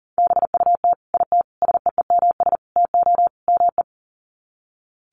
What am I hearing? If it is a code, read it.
Morse code, heard as a keyed tone, about 40 wpm, 700 Hz. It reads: BUT IT SEEMS TO ME
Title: Sets of 5 Words Encore2 40wpm